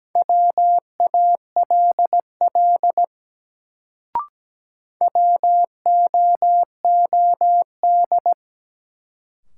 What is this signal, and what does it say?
Morse code, heard as a keyed tone, about 17 wpm, 700 Hz. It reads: WALL WOOD
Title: Sentences from Top 1100 Words - New Word Drill Alt - 17wpm